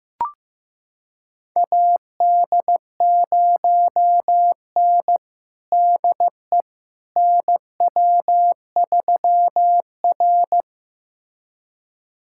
Morse code, heard as a keyed tone, about 15 wpm, 700 Hz. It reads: AD0N DE NW3R
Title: USA Callsign DE Callsign - 4 Characters - 15wpm